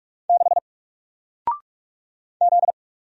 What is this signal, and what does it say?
Morse code, heard as a keyed tone, about 46 wpm, 700 Hz. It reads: 6 Z
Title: Single Character (Letter:Number:Punctuation:Prosign) Alt 45wpm